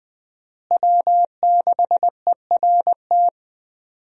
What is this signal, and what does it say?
Morse code, heard as a keyed tone, about 21 wpm, 700 Hz. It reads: W6ERT
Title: USA Call Signs - 5 Characters - Repeated 2x - 20wpm